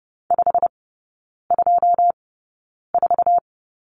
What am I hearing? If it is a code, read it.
Morse code, heard as a keyed tone, about 30 wpm, spaced as for 9 wpm, 700 Hz. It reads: 524
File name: course / 30:10wpm / 087 Lesson 15e - Number 5 - Three Characters 30wpm